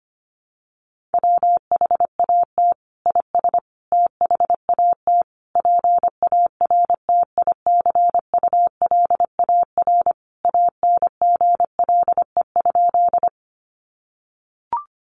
Morse code, heard as a keyed tone, about 25 wpm, 700 Hz. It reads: WHAT IS THAT PARTICULAR ANGLE?
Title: Sentences from Top 1500 Words 25wpm